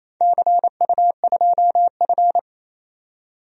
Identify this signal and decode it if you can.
Morse code, heard as a keyed tone, about 28 wpm, 700 Hz. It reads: CU2F
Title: DXCC-100 Call Signs - 4 Characters - 28wpm